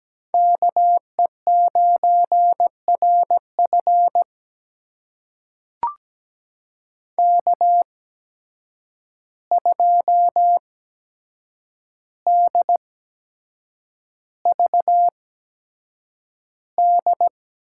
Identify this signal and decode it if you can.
Morse code, heard as a keyed tone, about 17 wpm, spaced as for 7 wpm, 700 Hz. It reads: KE9RF K2DVD
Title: USA Call Signs - 5 Characters - 8x Character Spacing - 17wpm